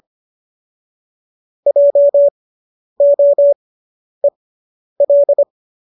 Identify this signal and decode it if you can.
Morse code, heard as a keyed tone, about 25 wpm, spaced as for 10 wpm, 570 Hz. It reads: JOEL